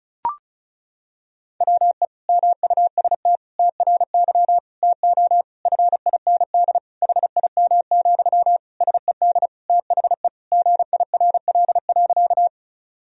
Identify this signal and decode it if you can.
Morse code, heard as a keyed tone, about 34 wpm, 700 Hz. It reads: WE MUST TRY TO FIND HIM, SED THE GIRL.
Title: Sentences from Top 400 Words and Abbreviations 35wpm